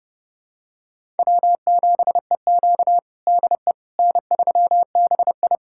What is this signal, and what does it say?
Morse code, heard as a keyed tone, about 30 wpm, 700 Hz. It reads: W7EQ DE N3BI
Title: USA Callsign DE Callsign - 4 Characters - 30wpm